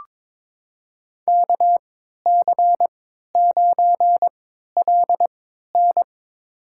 Morse code, heard as a keyed tone, about 21 wpm, spaced as for 12 wpm, 700 Hz. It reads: KC9LN
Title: USA Call Signs - 5 Characters - 3x Character Spacing - 22wpm